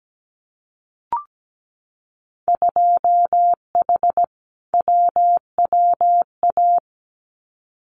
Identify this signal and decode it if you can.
Morse code, heard as a keyed tone, about 17 wpm, 700 Hz. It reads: 2H WWA
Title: Winter Field Day Exchanges 17wpm